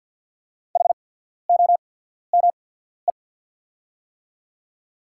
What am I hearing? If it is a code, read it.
Morse code, heard as a keyed tone, about 48 wpm, spaced as for 14 wpm, 690 Hz. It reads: HOME
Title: 8x Character Spacing - Top 300 Words - 50wpm